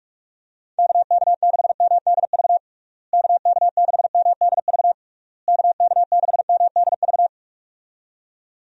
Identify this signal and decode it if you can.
Morse code, heard as a keyed tone, about 42 wpm, 700 Hz. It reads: KK6MDV KK6MDV KK6MDV
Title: USA Call Signs - 6 Characters - Repeated 3x - 45wpm